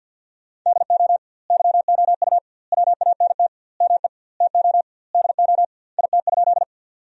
Morse code, heard as a keyed tone, about 49 wpm, 700 Hz. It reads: DO YOU WANT ME TO DO IT?
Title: 2x Word-Spacing - Sentences from Top 200 Words - 50wpm